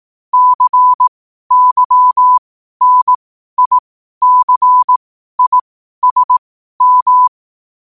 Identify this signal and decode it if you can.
Morse code, heard as a keyed tone, about 18 wpm, spaced as for 12 wpm, 1000 Hz. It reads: CYNICISM
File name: qotdfiles / 12wpm